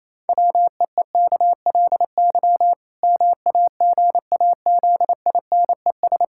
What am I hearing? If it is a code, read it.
Morse code, heard as a keyed tone, about 28 wpm, 700 Hz. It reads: WEEKLY MAGAZINES
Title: Pangram Challenge v2 - 28wpm